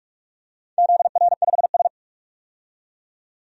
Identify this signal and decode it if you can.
Morse code, heard as a keyed tone, about 43 wpm, 700 Hz. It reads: ZR5S
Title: DXCC-100 Call Signs - 4 Characters - 45wpm